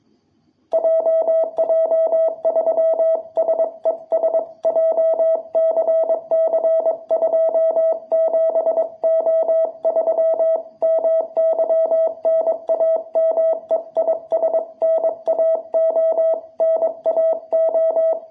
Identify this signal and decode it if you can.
Morse code, heard as a keyed tone, about 22 wpm, 660 Hz. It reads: JJ3SES JCC27O3 MY NAME IS NAO NAO
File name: jh8cbh / 2021cw